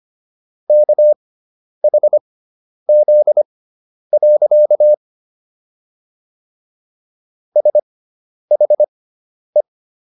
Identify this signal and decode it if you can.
Morse code, heard as a keyed tone, about 25 wpm, spaced as for 10 wpm, 600 Hz. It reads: KHZ. SHE